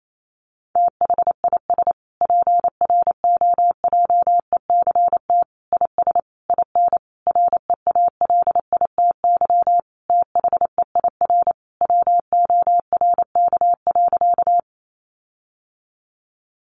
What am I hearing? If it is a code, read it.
Morse code, heard as a keyed tone, about 28 wpm, 700 Hz. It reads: THIS PROJECT IS IN REALITY THEIR WORK.
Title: Sentences from Top 1400 Words 28wpm